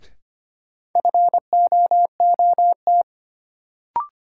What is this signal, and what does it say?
Morse code, heard as a keyed tone, about 25 wpm, 700 Hz. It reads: FOOT
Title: Sets of 5 Words - New Word Drill - 25wpm